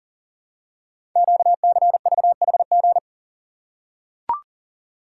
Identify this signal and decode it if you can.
Morse code, heard as a keyed tone, about 42 wpm, 690 Hz. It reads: QCVHG
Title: Code Groups - 5 Letters - 40wpm